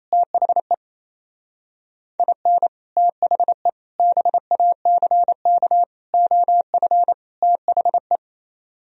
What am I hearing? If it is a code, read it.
Morse code, heard as a keyed tone, about 35 wpm, 700 Hz. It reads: THE IN THE BACK OF THE
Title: Speed-Racing - Sets of 5 Words 28wpm